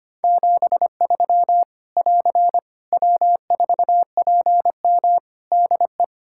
Morse code, heard as a keyed tone, about 25 wpm, 700 Hz. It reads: 73 <AR> W4PM DE
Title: K1USN SST QSO Practice 25wpm